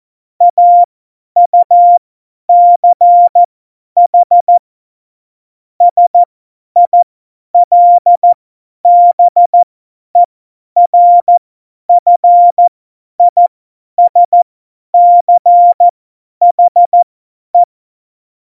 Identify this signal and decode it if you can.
Morse code, heard as a keyed tone, about 14 wpm, spaced as for 10 wpm, 700 Hz. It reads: AUCH SILBERFISCHE
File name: Morsekurs / Witze / Witz 16. Oktober 2018, 50 F70